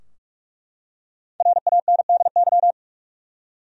Morse code, heard as a keyed tone, about 42 wpm, 700 Hz. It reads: RANDY
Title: USA Common Names - Max 5 Letters - 45wpm